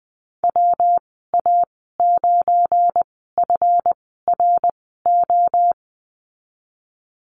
Morse code, heard as a keyed tone, about 20 wpm, spaced as for 14 wpm, 700 Hz. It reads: WA9FRO
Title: 213 Lesson 34g - Number 8 - Call Signs 20wpm